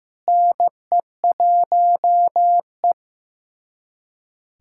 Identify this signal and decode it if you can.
Morse code, heard as a keyed tone, about 15 wpm, 700 Hz. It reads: NE1E
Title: USA Callsign DE Callsign - 4 Characters - 15wpm